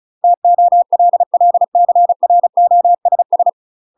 Morse code, heard as a keyed tone, about 34 wpm, 700 Hz. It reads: TOLLCROSS
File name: UK MASTER QSOs TRAINING/35WPM